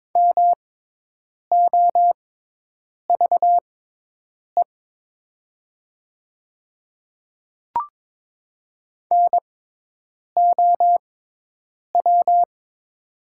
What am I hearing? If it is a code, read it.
Morse code, heard as a keyed tone, about 22 wpm, spaced as for 8 wpm, 700 Hz. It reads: MOVE NOW
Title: 6x Character Spacing - Top 200 Words - 22wpm